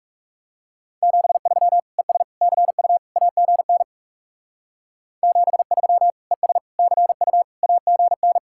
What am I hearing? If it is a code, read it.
Morse code, heard as a keyed tone, about 36 wpm, 700 Hz. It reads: 73 ES CU AGN 73 ES CU AGN
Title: Speed-Racing Common QSO Phrases Encore - 30wpm